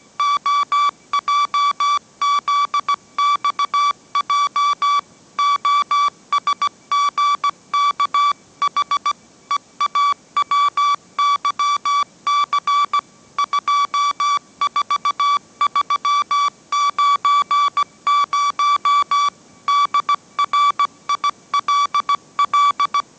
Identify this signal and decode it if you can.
Morse code, heard as a keyed tone, about 17 wpm, 1150 Hz. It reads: OJZXJ OSGKH EAWYC 24390 DRILL
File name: mrx / mrx080 18wpm